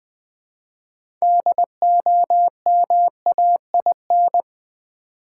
Morse code, heard as a keyed tone, about 20 wpm, 700 Hz. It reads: DOMAIN
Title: 6 Letter Words - First 3 Letters Given - 20wpm